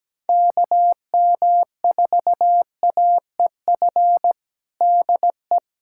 Morse code, heard as a keyed tone, about 17 wpm, 700 Hz. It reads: KM4AEF DE
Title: USA Callsign DE Callsign - 6 Characters - 17wpm